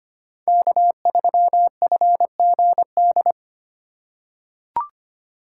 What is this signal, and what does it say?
Morse code, heard as a keyed tone, about 25 wpm, 700 Hz. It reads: K3FGD
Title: USA Call Signs - 3x Character Spacing - 25wpm